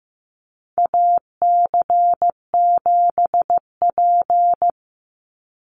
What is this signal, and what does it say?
Morse code, heard as a keyed tone, about 15 wpm, 700 Hz. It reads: AC7P
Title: NA SOTA Chaser Callsigns 15wpm